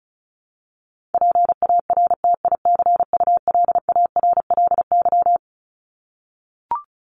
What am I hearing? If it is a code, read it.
Morse code, heard as a keyed tone, about 34 wpm, 700 Hz. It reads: PARTICULARLY
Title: LY Suffix - Top 100 Words with Base Words 35wpm